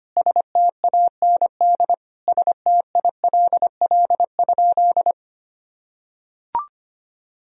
Morse code, heard as a keyed tone, about 25 wpm, 700 Hz. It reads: STAND STILL?
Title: Sentences from Top 1000 Words - Review Part 3 - 25wpm